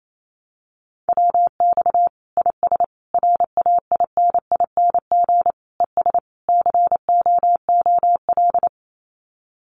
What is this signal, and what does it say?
Morse code, heard as a keyed tone, about 28 wpm, 700 Hz. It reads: WX IS RAINING ES COOL
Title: Common QSO Phrases 28wpm